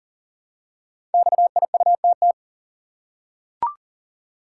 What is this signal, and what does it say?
Morse code, heard as a keyed tone, about 42 wpm, 690 Hz. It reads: XIUTT
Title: Code Groups - 5 Letters - 40wpm